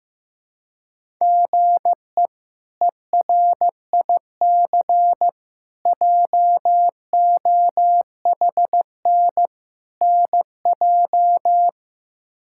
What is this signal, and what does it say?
Morse code, heard as a keyed tone, about 16 wpm, 700 Hz. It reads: GE ERIC JOHN NJ
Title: K1USN SST QSO Practice - 15wpm